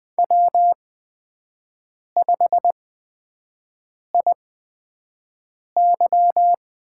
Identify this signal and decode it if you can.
Morse code, heard as a keyed tone, about 20 wpm, spaced as for 5 wpm, 700 Hz. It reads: W5IY
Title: USA Call Signs - 4 Characters - 8x Character Spacing - 20wpm